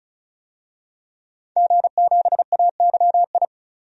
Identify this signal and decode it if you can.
Morse code, heard as a keyed tone, about 35 wpm, 700 Hz. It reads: G7AYI
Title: DXCC-100 Call Signs - 5 Characters - 35wpm